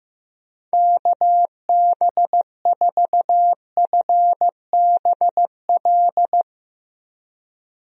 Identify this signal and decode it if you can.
Morse code, heard as a keyed tone, about 15 wpm, 700 Hz. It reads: KB4FBL
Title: USA Call Signs - 6 Characters - 15wpm